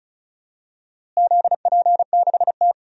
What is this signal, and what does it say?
Morse code, heard as a keyed tone, about 36 wpm, 700 Hz. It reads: ZP6T
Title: DXCC-100 Call Signs - 4 Characters - Repeated 2x - 35wpm